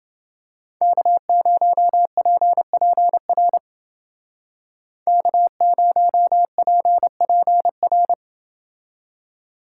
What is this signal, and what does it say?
Morse code, heard as a keyed tone, about 29 wpm, 700 Hz. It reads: K0PPR K0PPR